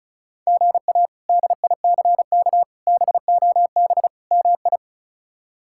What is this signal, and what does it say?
Morse code, heard as a keyed tone, about 34 wpm, 700 Hz. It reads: GA DICK BOB MI
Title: K1USN SST QSO Practice - 35wpm